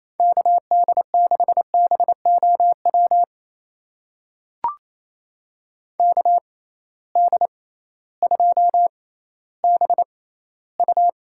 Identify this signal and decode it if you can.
Morse code, heard as a keyed tone, about 28 wpm, spaced as for 15 wpm, 700 Hz. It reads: KD6BOW KD2BU
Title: USA Call Signs - 6 Characters - 6x Character Spacing - 28wpm